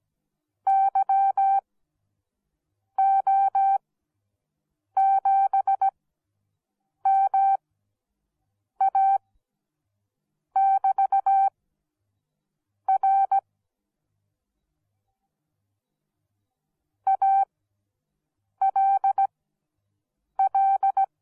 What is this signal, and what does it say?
Morse code, heard as a keyed tone, about 17 wpm, spaced as for 6 wpm, 780 Hz. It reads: YO7MA<BT>R ALL